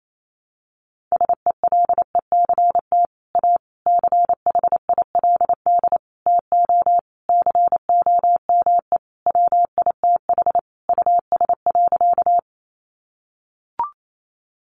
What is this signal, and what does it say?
Morse code, heard as a keyed tone, about 28 wpm, 700 Hz. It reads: SELECT A CHILD TO COME WITH US.
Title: Sentences from Top 1300 Words 28wpm